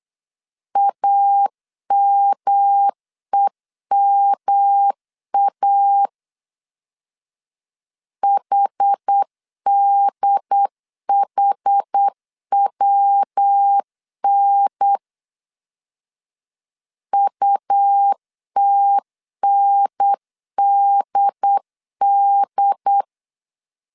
Morse code, heard as a keyed tone, about 8 wpm, 800 Hz. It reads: AMEMA HDHWN UTNDD
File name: CW / DET / 1 3 1500C S B 8wpm 4141